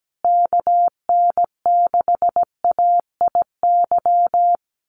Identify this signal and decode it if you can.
Morse code, heard as a keyed tone, about 17 wpm, 700 Hz. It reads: KN6AIY
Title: USA Call Signs Repeated 2x 17wpm